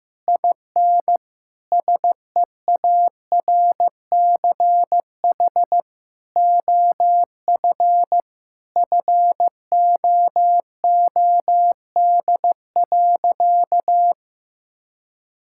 Morse code, heard as a keyed tone, about 15 wpm, 700 Hz. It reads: IN SEARCH OF FOOD.
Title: Sentences from Top 1000 Words - Review Part 5 - 15wpm